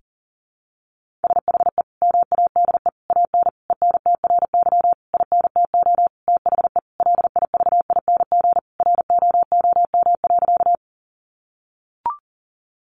Morse code, heard as a keyed tone, about 40 wpm, 700 Hz. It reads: SHE MADE AN ENTRY INTO THE LIVING ROOM.
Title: Sentences from Top 1500 Words 40wpm